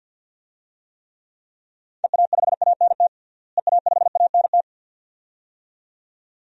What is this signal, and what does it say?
Morse code, heard as a keyed tone, about 51 wpm, 700 Hz. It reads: EA5ANT EA5ANT
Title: DXCC-100 Call Signs - 6 Characters - Repeated 2x - 50wpm